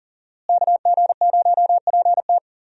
Morse code, heard as a keyed tone, about 40 wpm, 700 Hz. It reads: KG0PT
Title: USA Call Signs - 5 Characters - Repeated 3x - 40wpm